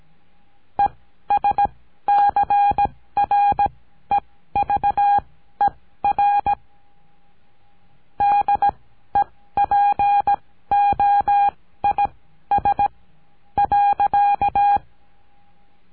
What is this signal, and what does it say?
Morse code, heard as a keyed tone, about 18 wpm, spaced as for 12 wpm, 810 Hz. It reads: ESCREVER DEPOIS.
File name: CURSO DE TELEGRAFIA